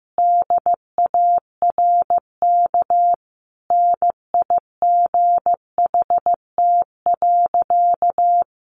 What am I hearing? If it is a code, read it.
Morse code, heard as a keyed tone, about 15 wpm, 700 Hz. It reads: DARK NIGHT.
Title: Sentences from Top 500 Words 15wpm